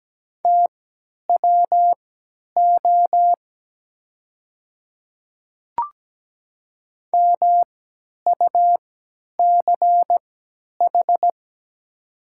Morse code, heard as a keyed tone, about 17 wpm, spaced as for 10 wpm, 700 Hz. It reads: TWO MUCH
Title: 3x Character Spacing - Top 200 Words - 17wpm